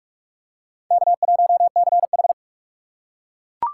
Morse code, heard as a keyed tone, about 41 wpm, 700 Hz. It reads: K1CH
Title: USA Call Signs - 4x Character Spacing - 45wpm